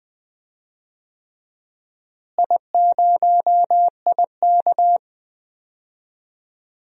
Morse code, heard as a keyed tone, about 20 wpm, 700 Hz. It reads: I0IK
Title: DXCC-100 Call Signs - 4 Characters - Repeated 3x - 20wpm